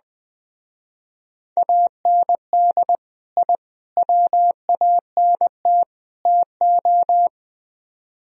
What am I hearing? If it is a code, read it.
Morse code, heard as a keyed tone, about 22 wpm, 700 Hz. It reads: AND I WANT TO
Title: Speed-Racing - Sets of 4 Words 20wpm